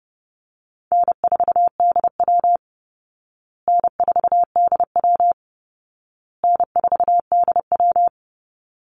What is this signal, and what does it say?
Morse code, heard as a keyed tone, about 30 wpm, 700 Hz. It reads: N4DW N4DW N4DW
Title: Call Signs Repeated 2x - 30wpm